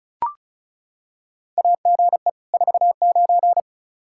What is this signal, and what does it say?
Morse code, heard as a keyed tone, about 36 wpm, 700 Hz. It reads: AGE 49
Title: K1USN SST QSO Practice 35wpm